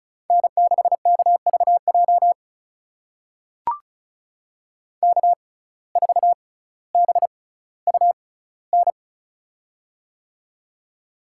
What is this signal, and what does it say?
Morse code, heard as a keyed tone, about 34 wpm, spaced as for 18 wpm, 700 Hz. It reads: N6KVJ K4BUN
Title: USA Call Signs - 5 Characters - 6x Character Spacing - 35wpm